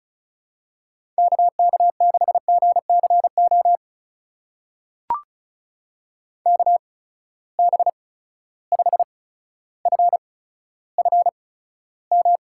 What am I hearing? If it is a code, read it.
Morse code, heard as a keyed tone, about 36 wpm, spaced as for 15 wpm, 700 Hz. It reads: KK6GCO KB5FFM
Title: USA Call Signs - 6 Characters - 8x Character Spacing - 35wpm